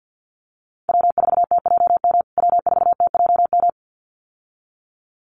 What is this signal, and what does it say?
Morse code, heard as a keyed tone, about 50 wpm, 700 Hz. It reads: W4TJM W4TJM
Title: USA Call Signs Repeated 2x 50wpm